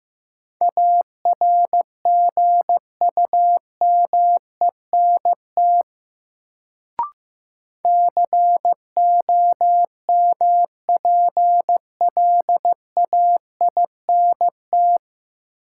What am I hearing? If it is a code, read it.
Morse code, heard as a keyed tone, about 15 wpm, 700 Hz. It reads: ARGUMENT COMPLAINT
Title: Sentences from Top 1400 Words - New Word Drill Alt - 15wpm